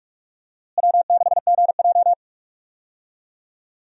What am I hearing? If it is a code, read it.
Morse code, heard as a keyed tone, about 41 wpm, 700 Hz. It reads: W6GJ
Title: USA Call Signs - 4 Characters - Repeated 2x - 45wpm